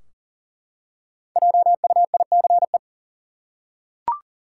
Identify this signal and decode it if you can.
Morse code, heard as a keyed tone, about 40 wpm, 700 Hz. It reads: JUICE
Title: Five Letter Words - 40wpm